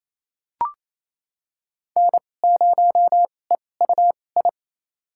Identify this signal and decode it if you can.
Morse code, heard as a keyed tone, about 28 wpm, spaced as for 20 wpm, 700 Hz. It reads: N0EUI
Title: USA Call Signs - 5 Characters - 2x Character Spacing - 28wpm